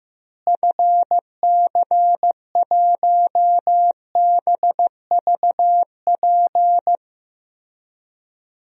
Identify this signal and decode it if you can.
Morse code, heard as a keyed tone, about 15 wpm, 700 Hz. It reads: FC1BVP
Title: DXCC-100 Call Signs - 6 Characters - Repeated 3x - 15wpm